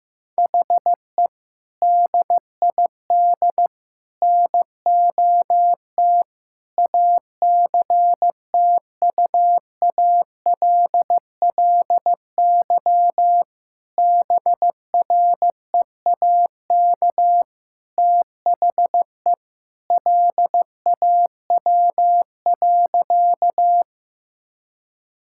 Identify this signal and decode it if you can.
Morse code, heard as a keyed tone, about 16 wpm, 700 Hz. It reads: HE DID NOT ACTUALLY BREAK THE LAW.
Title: Sentences from Top 1000 Words - Review Part 6 - 15wpm